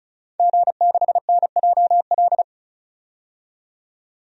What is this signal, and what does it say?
Morse code, heard as a keyed tone, about 34 wpm, 700 Hz. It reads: G6NJL
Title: DXCC-100 Call Signs - 5 Characters - 35wpm